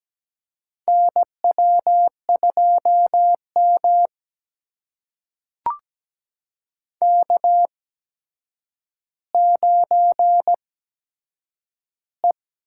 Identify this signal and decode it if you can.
Morse code, heard as a keyed tone, about 17 wpm, spaced as for 8 wpm, 700 Hz. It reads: NW2M K9E
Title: USA Call Signs - 4 Characters - 8x Character Spacing - 17wpm